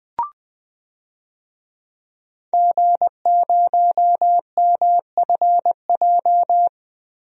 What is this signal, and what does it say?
Morse code, heard as a keyed tone, about 21 wpm, 700 Hz. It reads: G0MFJ
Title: DXCC-100 Call Signs - 5 Characters - Repeated 2x - 20wpm